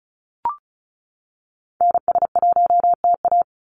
Morse code, heard as a keyed tone, about 34 wpm, 700 Hz. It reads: NS1TA
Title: NA SOTA Activator Callsigns 35wpm